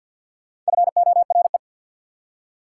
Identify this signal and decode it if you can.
Morse code, heard as a keyed tone, about 52 wpm, 690 Hz. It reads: FORE
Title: Prefixes and Suffixes 50wpm